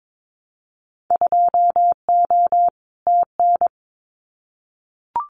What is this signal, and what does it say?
Morse code, heard as a keyed tone, about 22 wpm, 700 Hz. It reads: 2O TN
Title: Winter Field Day Exchanges 22wpm